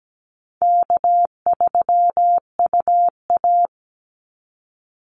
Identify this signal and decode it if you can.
Morse code, heard as a keyed tone, about 17 wpm, 700 Hz. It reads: K3UA
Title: Easy Call Signs - 17wpm